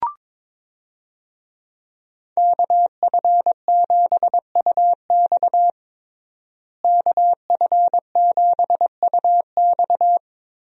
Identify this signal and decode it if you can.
Morse code, heard as a keyed tone, about 22 wpm, 700 Hz. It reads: KF7UX KF7UX
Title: USA Call Signs - 5 Characters - Repeated 3x - 22wpm